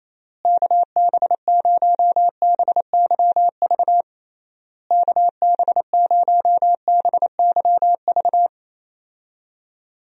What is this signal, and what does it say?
Morse code, heard as a keyed tone, about 28 wpm, 700 Hz. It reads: KB0BYV KB0BYV
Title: USA Call Signs - 6 Characters - Repeated 3x - 28wpm